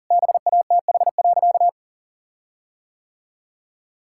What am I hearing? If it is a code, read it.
Morse code, heard as a keyed tone, about 40 wpm, 700 Hz. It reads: BATH.
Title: Sentences from Top 1600 Words 40wpm